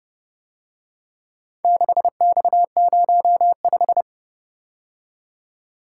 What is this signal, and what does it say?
Morse code, heard as a keyed tone, about 30 wpm, 700 Hz. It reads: 6X05